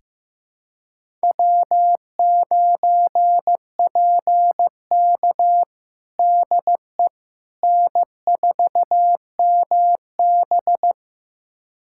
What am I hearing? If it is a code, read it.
Morse code, heard as a keyed tone, about 16 wpm, 700 Hz. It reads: W9PK DE N4MB
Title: USA Callsign DE Callsign - 4 Characters - 15wpm